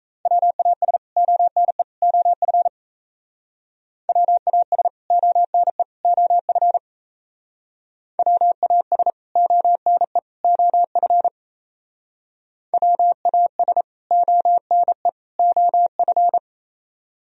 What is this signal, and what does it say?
Morse code, heard as a keyed tone, about 34 wpm, 700 Hz. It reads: WAS ONE OF WAS ONE OF WAS ONE OF WAS ONE OF
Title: Speed-Racing - Sets of 3 Words 28wpm